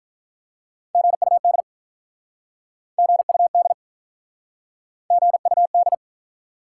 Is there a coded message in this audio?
Yes: Morse code from 0.9 to 5.9 s, heard as a keyed tone, about 47 wpm, 700 Hz. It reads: GUD GUD GUD